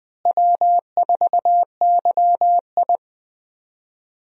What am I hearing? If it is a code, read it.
Morse code, heard as a keyed tone, about 20 wpm, 700 Hz. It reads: W4YI
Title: USA Call Signs - 4 Characters - Repeated 2x - 20wpm